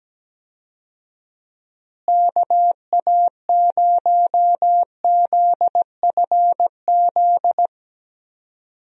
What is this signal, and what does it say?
Morse code, heard as a keyed tone, about 17 wpm, 700 Hz. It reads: KA0ZFZ
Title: USA Call Signs - 6 Characters - Repeated 2x - 17wpm